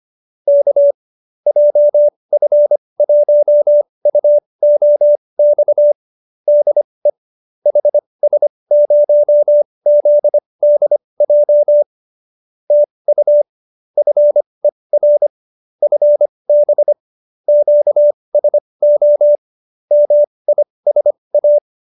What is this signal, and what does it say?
Morse code, heard as a keyed tone, about 25 wpm, spaced as for 20 wpm, 590 Hz. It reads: K JF1UOX DE HS0ZDJ TU FER FB QSO MISA